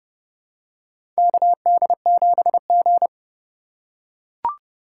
Morse code, heard as a keyed tone, about 30 wpm, 700 Hz. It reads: KD7G
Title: USA Call Signs - 6x Character Spacing - 30wpm